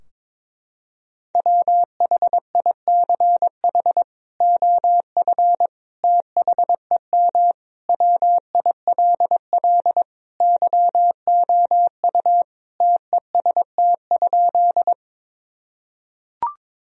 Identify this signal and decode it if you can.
Morse code, heard as a keyed tone, about 21 wpm, 700 Hz. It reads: WHICH OF THEM WILL YOU TEST?
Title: Sentences from Top 1000 Words - Review Part 3 - 22wpm